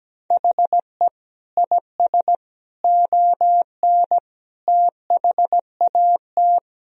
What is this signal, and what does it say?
Morse code, heard as a keyed tone, about 17 wpm, 700 Hz. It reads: HE IS ON THAT
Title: Sentences from Top 1000 Words - Review Part 1 - 17wpm